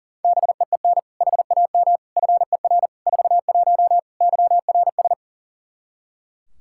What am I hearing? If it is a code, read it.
Morse code, heard as a keyed tone, about 40 wpm, 700 Hz. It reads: BEEN HAM FER 41 YRS
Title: K1USN SST QSO Practice 40wpm